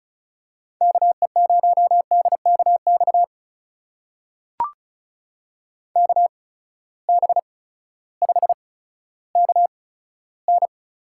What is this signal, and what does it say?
Morse code, heard as a keyed tone, about 36 wpm, spaced as for 16 wpm, 700 Hz. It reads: KE0DKX KB5KN
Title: USA Call Signs - 6 Characters - 8x Character Spacing - 35wpm